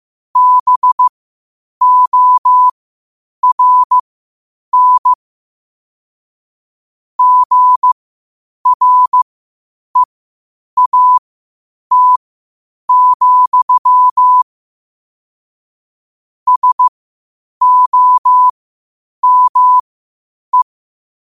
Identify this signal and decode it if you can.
Morse code, heard as a keyed tone, about 15 wpm, spaced as for 8 wpm, 1000 Hz. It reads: BORN GREAT, SOME
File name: qotdfiles / 8wpm